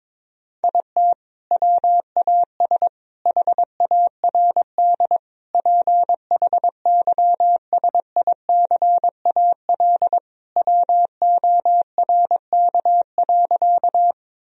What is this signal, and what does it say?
Morse code, heard as a keyed tone, about 22 wpm, 700 Hz. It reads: IT WAS HARD PHYSICAL WORK.
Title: Sentences from Top 1000 Words - Review Part 4 - 22wpm